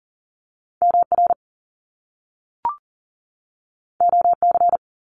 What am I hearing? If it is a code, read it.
Morse code, heard as a keyed tone, about 41 wpm, 700 Hz. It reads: MR OC
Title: Top Two Letter Combinations in Words 40wpm